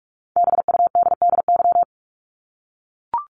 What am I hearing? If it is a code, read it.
Morse code, heard as a keyed tone, about 44 wpm, 700 Hz. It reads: BUDDY
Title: Top 1751-2000 Words 45wpm